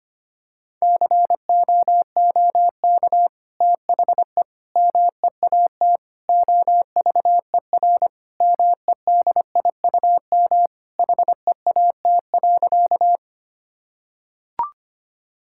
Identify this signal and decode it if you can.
Morse code, heard as a keyed tone, about 25 wpm, 700 Hz. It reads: COOK THE MEAT OVER MEDIUM HEAT.
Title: Sentences from Top 1000 Words - Review Part 6 - 25wpm